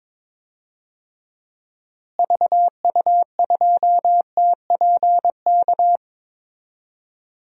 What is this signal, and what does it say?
Morse code, heard as a keyed tone, about 22 wpm, 700 Hz. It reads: VU2TPK
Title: DXCC-100 Call Signs - 6 Characters - Repeated 3x - 22wpm